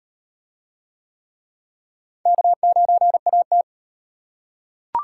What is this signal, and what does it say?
Morse code, heard as a keyed tone, about 38 wpm, 700 Hz. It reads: K9AT